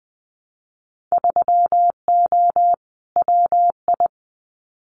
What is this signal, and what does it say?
Morse code, heard as a keyed tone, about 20 wpm, 700 Hz. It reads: 3O WI
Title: Winter Field Day Exchanges 20wpm